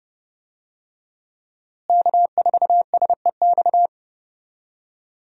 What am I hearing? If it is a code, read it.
Morse code, heard as a keyed tone, about 30 wpm, 700 Hz. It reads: K4SEX
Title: USA Call Signs - 5 Characters - Repeated 3x - 30wpm